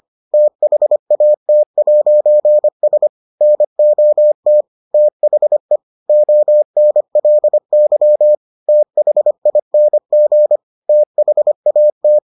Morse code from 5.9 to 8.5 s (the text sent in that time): ONLY